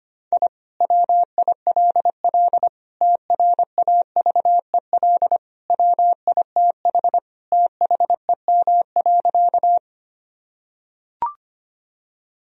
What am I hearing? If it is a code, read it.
Morse code, heard as a keyed tone, about 25 wpm, 700 Hz. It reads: I WILL TRAVEL WITH THEM.
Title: Sentences from Top 1000 Words - Review Part 2 - 25wpm